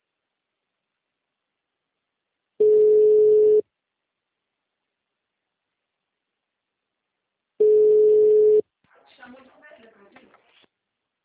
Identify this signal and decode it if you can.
Ringback tone (Europe, 425 Hz), with speech.